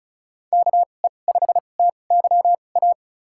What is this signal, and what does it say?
Morse code, heard as a keyed tone, about 36 wpm, spaced as for 25 wpm, 700 Hz. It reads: KE5TYA
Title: USA Call Signs - 6 Characters - 2x Character Spacing - 35wpm